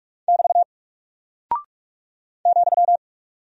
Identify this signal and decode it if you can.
Morse code, heard as a keyed tone, about 44 wpm, 700 Hz. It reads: <BT> ,
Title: Single Character (Letter:Number:Punctuation:Prosign) Alt 45wpm